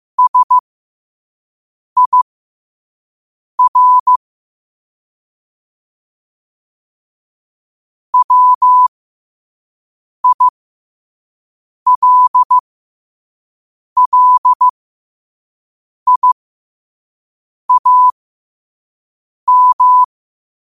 Morse code, heard as a keyed tone, about 15 wpm, spaced as for 5 wpm, 1000 Hz. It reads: SIR WILLIAM
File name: qotdfiles / 5wpm